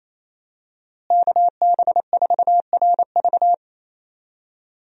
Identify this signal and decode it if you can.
Morse code, heard as a keyed tone, about 27 wpm, 700 Hz. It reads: KB4RV
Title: USA Call Signs - 5 Characters - Repeated 3x - 28wpm